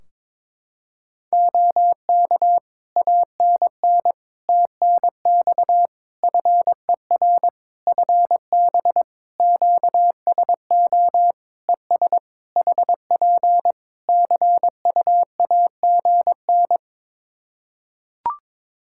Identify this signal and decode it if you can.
Morse code, heard as a keyed tone, about 22 wpm, 700 Hz. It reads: OK ANN TNX FER FB QSO ES HP CUAGN
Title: Practice QSO - Exchange 4 - Ending - 22wpm